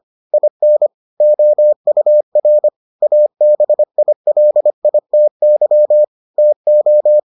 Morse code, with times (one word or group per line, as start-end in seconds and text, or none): 0.3-0.9 IN
1.2-2.7 OUR
3.0-6.0 ABILITY
6.4-7.2 TO